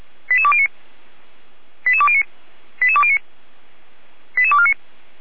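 Five-tone selective call (ZVEI), heard as quick runs of five tones, tones of about 70 ms. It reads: 88209 88209 88209 88249
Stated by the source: ZVEI five-tone call received on an SDR.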